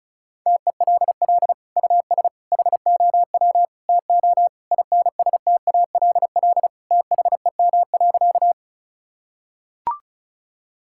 Morse code, heard as a keyed tone, about 36 wpm, 700 Hz. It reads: TELL US HOW TO INSTALL THEM.
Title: Sentences from Top 1600 Words 35wpm